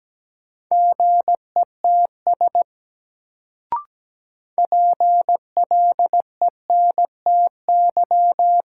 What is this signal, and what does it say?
Morse code, heard as a keyed tone, about 17 wpm, 700 Hz. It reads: GETS PLENTY
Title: Sentences from Top 1300 Words - New Word Drill - 17wpm